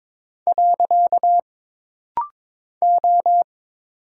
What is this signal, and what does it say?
Morse code, heard as a keyed tone, about 22 wpm, 700 Hz. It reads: . O
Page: This rapid-fire timing is geared to push your ICR (Instant Character Recognition) skills.